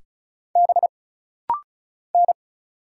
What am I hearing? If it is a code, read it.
Morse code, heard as a keyed tone, about 35 wpm, 700 Hz. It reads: B N
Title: Single Letters - Rapid-Fire 35wpm